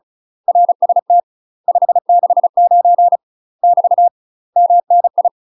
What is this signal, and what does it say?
Morse code, heard as a keyed tone, about 34 wpm, 700 Hz. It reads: RST 569 <BT> MNI
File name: UK MASTER QSOs TRAINING / 35WPM / UK MASTER QSOs 0021